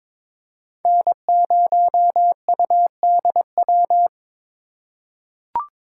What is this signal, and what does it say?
Morse code, heard as a keyed tone, about 22 wpm, 700 Hz. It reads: N0UDW
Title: USA Call Signs - 5 Characters - 2x Character Spacing - 22wpm